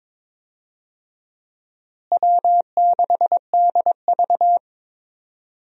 Morse code, heard as a keyed tone, about 22 wpm, 700 Hz. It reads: W6DV
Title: USA Call Signs - 4 Characters - Repeated 2x - 22wpm